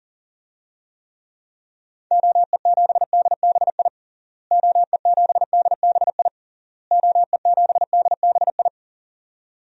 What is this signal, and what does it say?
Morse code, heard as a keyed tone, about 40 wpm, 700 Hz. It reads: OE7DBI OE7DBI OE7DBI
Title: DXCC-100 Call Signs - 6 Characters - Repeated 3x - 40wpm